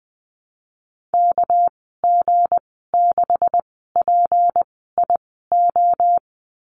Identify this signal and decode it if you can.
Morse code, heard as a keyed tone, about 19 wpm, spaced as for 14 wpm, 700 Hz. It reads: KG6PIO
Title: 200 Lesson 32g - Letter k - Call Signs 20wpm